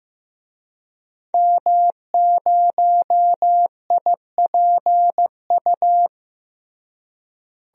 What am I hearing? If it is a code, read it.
Morse code, heard as a keyed tone, about 16 wpm, 700 Hz. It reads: M0IPU
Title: DXCC-100 Call Signs - 5 Characters - Repeated 2x - 15wpm